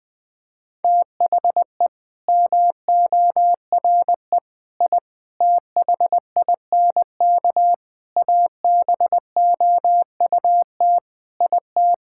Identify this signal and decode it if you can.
Morse code, heard as a keyed tone, about 20 wpm, 700 Hz. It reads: THE MORE I THINK ABOUT IT
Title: Sets of 8 Words - 20wpm